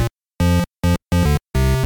There is no code.